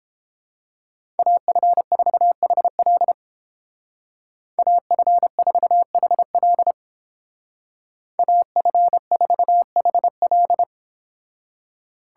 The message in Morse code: AF4HL AF4HL AF4HL

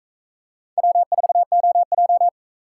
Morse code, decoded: W4OJ

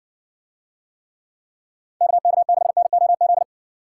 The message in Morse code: KK6NOZ

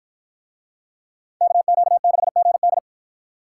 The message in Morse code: KQ6GD

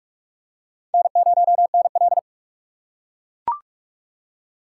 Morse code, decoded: N0NL

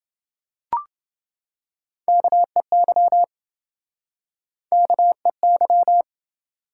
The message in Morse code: KEY KEY

